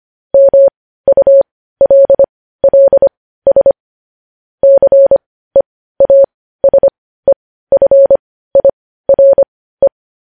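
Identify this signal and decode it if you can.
Morse code, heard as a keyed tone, about 25 wpm, spaced as for 15 wpm, 550 Hz. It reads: MULLS CEASEFIRE
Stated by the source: Morse practice of news headlines at 15 wpm.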